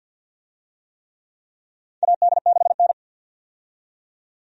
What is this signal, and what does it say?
Morse code, heard as a keyed tone, about 49 wpm, 700 Hz. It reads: AD6N
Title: USA Call Signs - 4 Characters - 50wpm